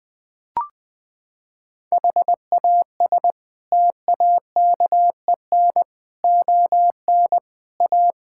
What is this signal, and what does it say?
Morse code, heard as a keyed tone, about 21 wpm, 700 Hz. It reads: HAS TAKEN ON A
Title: Sets of 8 Words - 20wpm